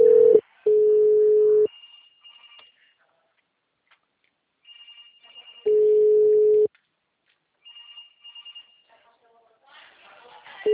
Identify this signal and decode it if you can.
Ringback tone (Europe, 425 Hz), with speech.